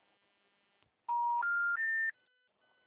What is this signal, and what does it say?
Special information tone.